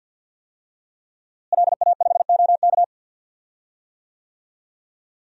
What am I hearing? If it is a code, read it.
Morse code, heard as a keyed tone, about 52 wpm, 700 Hz. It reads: LA5OK
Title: DXCC-100 Call Signs - 5 Characters - Repeated 3x - 50wpm